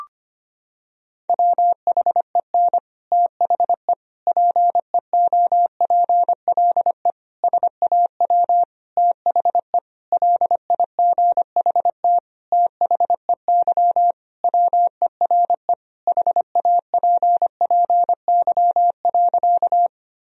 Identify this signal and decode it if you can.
Morse code, heard as a keyed tone, about 25 wpm, 700 Hz. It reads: WHEN THE PEOPLE SAW THE LIGHT THEY WERE HAPPY.
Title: Sentences from Top 1000 Words - Review Part 1 - 25wpm